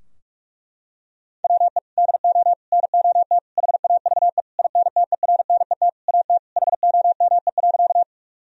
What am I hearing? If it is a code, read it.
Morse code, heard as a keyed tone, about 41 wpm, 700 Hz. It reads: WE DO NOT HAVE INTERNET AT HOME.
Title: Sentences from Top 1000 Words - Review Part 4 - 45wpm